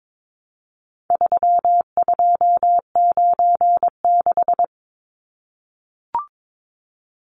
Morse code, heard as a keyed tone, about 22 wpm, 700 Hz. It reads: 3296
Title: Numbers - 4-Digit 22wpm